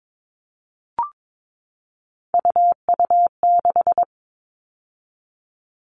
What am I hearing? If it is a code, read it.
Morse code, heard as a keyed tone, about 22 wpm, 700 Hz. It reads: UU6
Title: All Cut Numbers - 3-Digit 22wpm